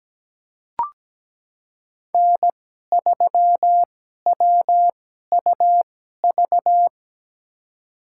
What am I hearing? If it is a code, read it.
Morse code, heard as a keyed tone, about 17 wpm, spaced as for 12 wpm, 700 Hz. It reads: N3WUV
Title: USA Call Signs - 5 Characters - 2x Character Spacing - 17wpm